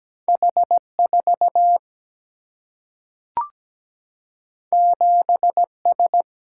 Morse code, heard as a keyed tone, about 17 wpm, 700 Hz. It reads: H4 7S
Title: Single-Letter Single-Number 17wpm